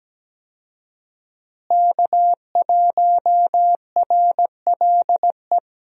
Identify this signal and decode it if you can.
Morse code, heard as a keyed tone, about 17 wpm, 700 Hz. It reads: K1RLE
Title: USA Call Signs - 5 Characters - Repeated 3x - 17wpm